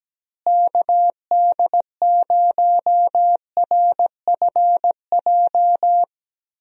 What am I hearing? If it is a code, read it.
Morse code, heard as a keyed tone, about 17 wpm, 700 Hz. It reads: KD0RFJ
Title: USA Call Signs - 6 Characters - Repeated 3x - 17wpm